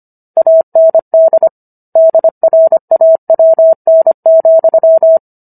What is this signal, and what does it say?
Morse code, heard as a keyed tone, about 25 wpm, 650 Hz. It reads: AND DRAWN,
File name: Book One / 16-Chapter 16 - 25 WPM